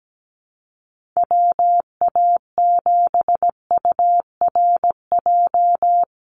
Morse code, heard as a keyed tone, about 17 wpm, 700 Hz. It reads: WA7URJ
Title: USA Call Signs 17wpm